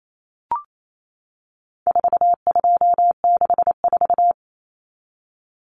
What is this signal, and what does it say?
Morse code, heard as a keyed tone, about 28 wpm, 700 Hz. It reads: 4264
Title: Common Cut Numbers - 4-Digit 28wpm